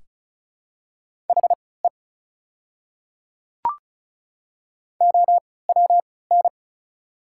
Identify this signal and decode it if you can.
Morse code, heard as a keyed tone, about 36 wpm, spaced as for 20 wpm, 700 Hz. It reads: HE OWN
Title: 3x Character Spacing - Top 500 Words - 35wpm